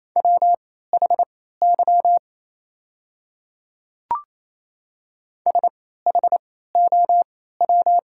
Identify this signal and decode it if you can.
Morse code, heard as a keyed tone, about 28 wpm, spaced as for 16 wpm, 700 Hz. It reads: WHY SHOW
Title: 3x Character Spacing - Top 200 Words - 28wpm